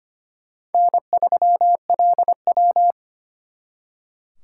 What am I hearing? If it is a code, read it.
Morse code, heard as a keyed tone, about 32 wpm, 700 Hz. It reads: N3LW